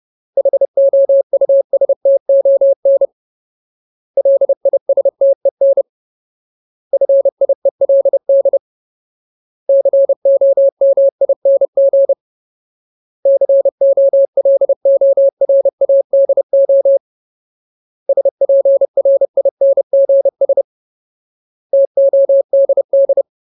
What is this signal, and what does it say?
Morse code, heard as a keyed tone, about 30 wpm, 550 Hz. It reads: HOUSTON LISTEN FIELD COMING COLORADO SPRINGS TODD
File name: Ultimate Word List / 30WPM Ultimate